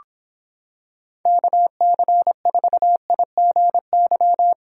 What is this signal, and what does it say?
Morse code, heard as a keyed tone, about 26 wpm, 700 Hz. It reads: KC4IGY